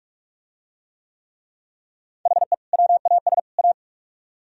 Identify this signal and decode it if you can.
Morse code, heard as a keyed tone, about 41 wpm, 700 Hz. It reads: HE WAS A